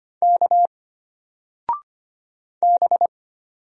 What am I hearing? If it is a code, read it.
Morse code, heard as a keyed tone, about 25 wpm, 700 Hz. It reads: K B
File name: Single-Letter-Number-25wpm